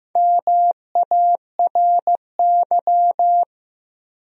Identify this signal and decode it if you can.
Morse code, heard as a keyed tone, about 15 wpm, 700 Hz. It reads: MARY
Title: USA Common Names - 15wpm